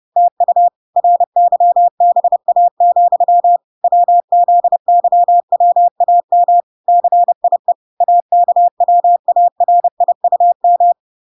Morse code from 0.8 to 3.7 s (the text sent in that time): RYBA,